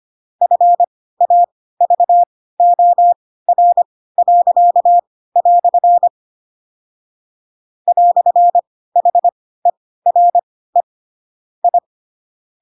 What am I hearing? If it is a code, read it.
Morse code, heard as a keyed tone, about 25 wpm, spaced as for 16 wpm, 700 Hz. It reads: FAVOR." "HERE I
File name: Pinocchio / Pinocchio ch2 10 25 ik1hge